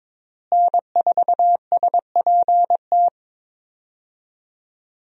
Morse code, heard as a keyed tone, about 23 wpm, 700 Hz. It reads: N4SPT
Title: USA Call Signs - 5 Characters - Repeated 2x - 22wpm